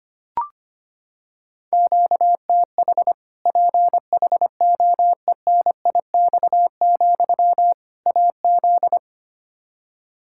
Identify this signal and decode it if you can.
Morse code, heard as a keyed tone, about 25 wpm, 700 Hz. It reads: QTH PHOENIX, AZ
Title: Practice QSO - Exchange 1 - Beginning - 25wpm